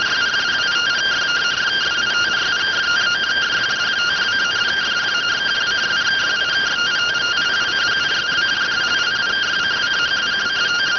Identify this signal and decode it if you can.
Two-tone teleprinter signal, about 170 Hz shift.